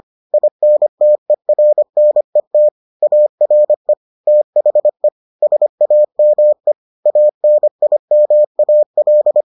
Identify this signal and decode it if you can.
Morse code, heard as a keyed tone, about 25 wpm, 600 Hz. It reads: INTERNET ARE THE SAME ANIMAL